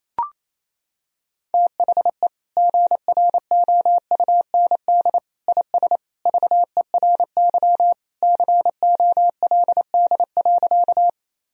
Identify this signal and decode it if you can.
Morse code, heard as a keyed tone, about 28 wpm, 700 Hz. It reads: THE GROUND IS VERY COLD.
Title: Sentences from Top 1000 Words - Review Part 2 - 28wpm